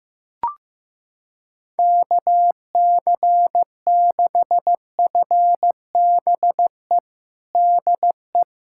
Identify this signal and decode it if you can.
Morse code, heard as a keyed tone, about 15 wpm, 700 Hz. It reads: KC6FBE DE
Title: USA Callsign DE Callsign - 6 Characters - 15wpm